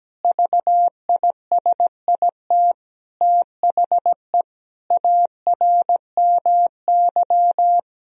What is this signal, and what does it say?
Morse code, heard as a keyed tone, about 17 wpm, 700 Hz. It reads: VISIT THE ARMY
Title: Sentences from Top 1000 Words - Review Part 2 - 17wpm